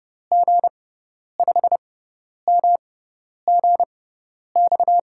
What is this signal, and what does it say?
Morse code, heard as a keyed tone, about 30 wpm, spaced as for 10 wpm, 700 Hz. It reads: G5MGX
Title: USA Call Signs - 6 Characters - 6x Character Spacing - 30wpm